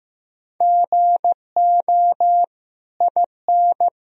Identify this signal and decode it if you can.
Morse code, heard as a keyed tone, about 15 wpm, 700 Hz. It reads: GO IN
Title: Sentences from Top 500 Words and Abbreviations 15wpm